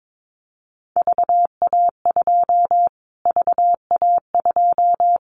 Morse code, heard as a keyed tone, about 22 wpm, 700 Hz. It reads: VA2 VA2
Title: Callsign Prefixes for Top 25 DXCC - Repeated 3x - 22wpm